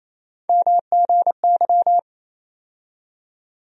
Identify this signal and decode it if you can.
Morse code, heard as a keyed tone, about 28 wpm, 700 Hz. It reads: MGY
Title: USA Call Sign Postfixes - 3 Letters 28wpm